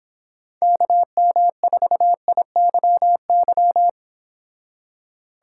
Morse code, heard as a keyed tone, about 25 wpm, 700 Hz. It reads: KM4IYY